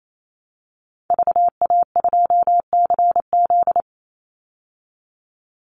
Morse code, heard as a keyed tone, about 28 wpm, 700 Hz. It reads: VA2CZ
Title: Call Signs Repeated 3x - 28wpm